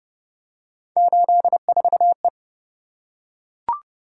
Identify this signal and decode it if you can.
Morse code, heard as a keyed tone, about 25 wpm, 700 Hz. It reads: 84E
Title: Speed-Racing - Three Letter-Number Combinations - 20wpm